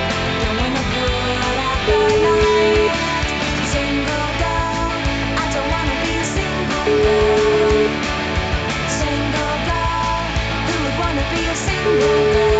Ringback tone (Europe, 425 Hz), with noise.